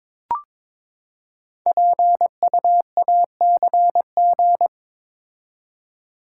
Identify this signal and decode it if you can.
Morse code, heard as a keyed tone, about 23 wpm, 700 Hz. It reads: PUACG
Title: Code Groups - 5 Letters - 22wpm